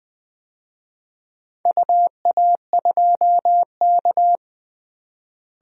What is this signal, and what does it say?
Morse code, heard as a keyed tone, about 20 wpm, 700 Hz. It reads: UA2K
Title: DXCC-100 Call Signs - 4 Characters - Repeated 3x - 20wpm